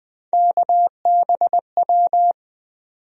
Morse code, heard as a keyed tone, about 20 wpm, 700 Hz. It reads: KBW
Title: USA Call Sign Postfixes - 3 Letters 20wpm